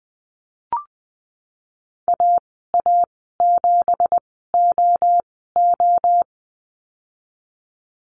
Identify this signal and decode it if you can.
Morse code, heard as a keyed tone, about 19 wpm, spaced as for 14 wpm, 700 Hz. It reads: AA7OO
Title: 167 Lesson 27g - Number 7 - Call Signs 20wpm